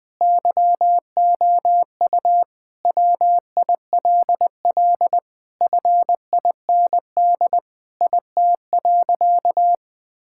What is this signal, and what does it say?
Morse code, heard as a keyed tone, about 20 wpm, 700 Hz. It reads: YOU WILL FIND IT.
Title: Sentences with 4-Word Phrases - 20wpm